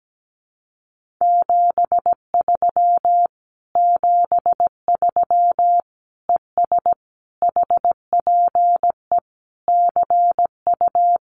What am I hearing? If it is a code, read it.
Morse code, heard as a keyed tone, about 17 wpm, 700 Hz. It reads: 73 73 ES HPE CU
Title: Common QSO Phrases Encore 17wpm